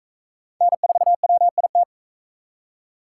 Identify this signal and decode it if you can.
Morse code, heard as a keyed tone, about 41 wpm, 700 Hz. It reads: N4WIT